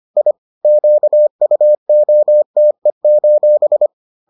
Morse code, heard as a keyed tone, about 25 wpm, 600 Hz. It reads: I QUOTE: